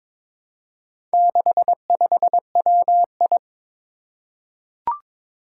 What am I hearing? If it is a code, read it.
Morse code, heard as a keyed tone, about 22 wpm, 700 Hz. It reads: 65WI
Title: Four Challenging Letter-Number Combinations 22wpm